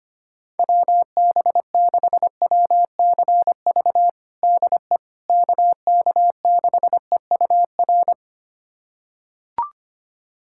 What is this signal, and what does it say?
Morse code, heard as a keyed tone, about 25 wpm, 700 Hz. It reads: WB6WCV DE KK6EUR
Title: USA Callsign DE Callsign - 6 Characters - 25wpm